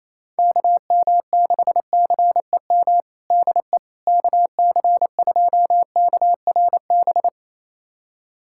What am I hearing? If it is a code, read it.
Morse code, heard as a keyed tone, about 28 wpm, 700 Hz. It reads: KM6CEM DE KC2KRB
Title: USA Callsign DE Callsign - 6 Characters - 28wpm